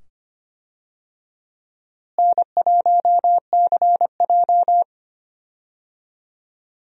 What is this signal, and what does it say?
Morse code, heard as a keyed tone, about 25 wpm, 700 Hz. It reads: N1CJ
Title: USA Call Signs - 4 Characters - Repeated 2x - 25wpm